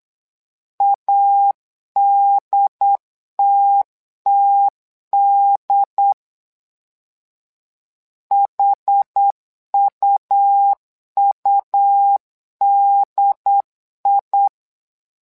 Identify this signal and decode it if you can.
Morse code, heard as a keyed tone, about 8 wpm, 800 Hz. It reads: ADTTD HUUDI
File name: LEZIONI / 2 UHDN / 2 UHD 8wpm 2258